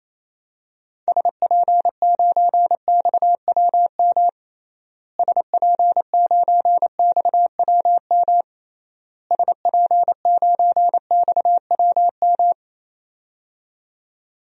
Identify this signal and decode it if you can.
Morse code, heard as a keyed tone, about 28 wpm, 700 Hz. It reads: SP9XWM SP9XWM SP9XWM
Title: DXCC-100 Call Signs - 6 Characters - Repeated 3x - 28wpm